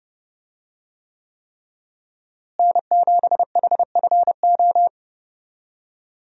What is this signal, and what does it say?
Morse code, heard as a keyed tone, about 31 wpm, 700 Hz. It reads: N7HFO
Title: USA Call Signs - 5 Characters - Repeated 3x - 30wpm